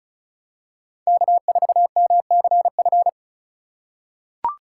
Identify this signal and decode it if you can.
Morse code, heard as a keyed tone, about 36 wpm, 700 Hz. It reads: K4MCF
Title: USA Call Signs - 5 Characters - 4x Character Spacing - 35wpm